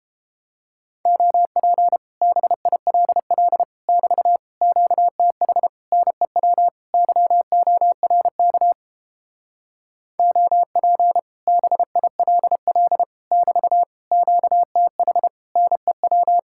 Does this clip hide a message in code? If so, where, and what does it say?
Yes: Morse code from 1.1 to 16.4 s, heard as a keyed tone, about 29 wpm, 700 Hz. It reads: OP BILL <BT> QTH NEW YORK OP BILL <BT> QTH NEW